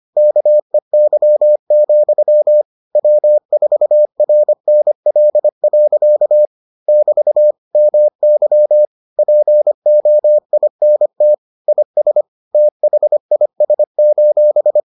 Morse code, heard as a keyed tone, about 25 wpm, 600 Hz. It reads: KEY, W4RNL. <BT> MY POINT IS THIS: